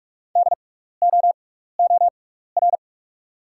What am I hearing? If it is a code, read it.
Morse code, heard as a keyed tone, about 47 wpm, spaced as for 16 wpm, 690 Hz. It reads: DOOR
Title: 6x Character Spacing - Top 300 Words - 45wpm